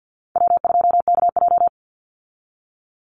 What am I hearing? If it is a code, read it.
Morse code, heard as a keyed tone, about 50 wpm, 700 Hz. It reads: W2KJ
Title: Call Signs - 50wpm